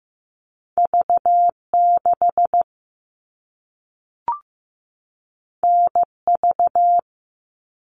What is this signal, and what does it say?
Morse code, heard as a keyed tone, about 15 wpm, 700 Hz. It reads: V6 NV
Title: All Cut Numbers - 2-Digit 15wpm